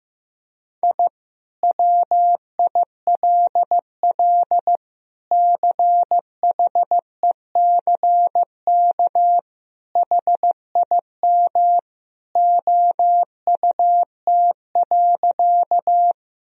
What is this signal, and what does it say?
Morse code, heard as a keyed tone, about 15 wpm, 700 Hz. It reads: I WILL CHECK HIM OUT.
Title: Sentences from Top 1000 Words - Review Part 3 - 15wpm